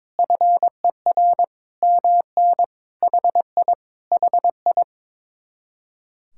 Morse code, heard as a keyed tone, about 22 wpm, 700 Hz. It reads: FER MN HI HI
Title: Speed-Racing Common QSO Phrases Encore - 22wpm